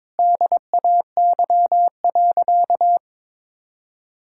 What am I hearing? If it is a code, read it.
Morse code, heard as a keyed tone, about 22 wpm, 700 Hz. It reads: DAY.
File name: practice / 22wpm / Sentences with 3-Word Phrases - 22wpm